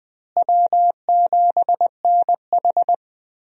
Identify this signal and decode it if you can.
Morse code, heard as a keyed tone, about 21 wpm, 700 Hz. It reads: W7NH
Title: USA Call Signs - 4 Characters - Repeated 3x - 20wpm